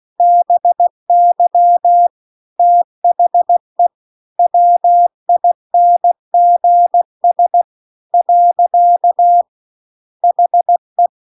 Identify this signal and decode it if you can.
Morse code, heard as a keyed tone, about 16 wpm, 700 Hz. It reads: BY THE WINGS . HE